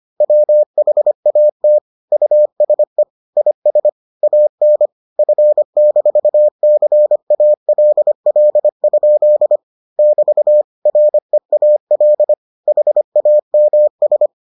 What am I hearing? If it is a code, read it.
Morse code, heard as a keyed tone, about 25 wpm, 600 Hz. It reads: WHAT USE IS AN F-CALL? <BT> REAL HAMS